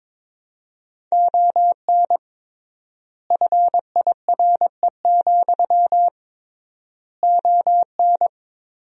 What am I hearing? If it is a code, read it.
Morse code, heard as a keyed tone, about 22 wpm, 700 Hz. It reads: ON FIRE, ON
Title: 3x Word-Spacing - Sentences from Top 400 Words - 22wpm